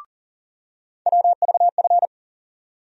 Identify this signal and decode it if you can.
Morse code, heard as a keyed tone, about 40 wpm, 700 Hz. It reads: WVF
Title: USA Call Sign Postfixes - 3 Letters 40wpm